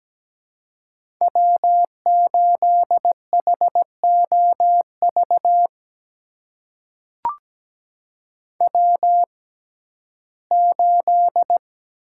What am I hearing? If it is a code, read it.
Morse code, heard as a keyed tone, about 17 wpm, spaced as for 12 wpm, 700 Hz. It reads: W8HOV W8
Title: USA Call Signs - 6x Character Spacing - 17wpm